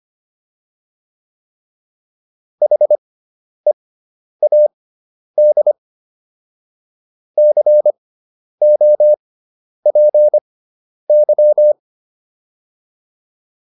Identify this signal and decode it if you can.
Morse code, heard as a keyed tone, about 25 wpm, spaced as for 10 wpm, 600 Hz. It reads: HEAD COPY